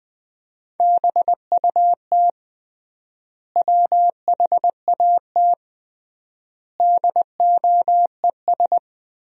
Morse code, heard as a keyed tone, about 20 wpm, 700 Hz. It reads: BUT WHAT DOES
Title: 3x Word-Spacing - Sentences from Top 200 Words - 20wpm